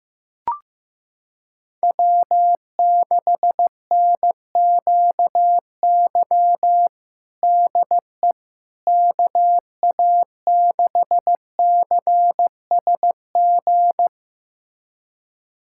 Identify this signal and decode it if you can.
Morse code, heard as a keyed tone, about 15 wpm, 700 Hz. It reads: W6NQY DE KA6CSG
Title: USA Callsign DE Callsign 15wpm